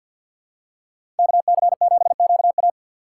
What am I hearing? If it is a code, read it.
Morse code, heard as a keyed tone, about 50 wpm, 700 Hz. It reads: KC7QA
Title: USA Call Signs - 6x Character Spacing - 50wpm